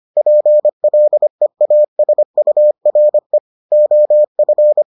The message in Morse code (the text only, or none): PLEASURE OF